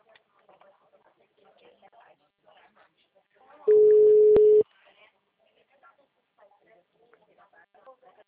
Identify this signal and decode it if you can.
Ringback tone (Europe, 425 Hz), with speech.
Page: Rio 3 - Ura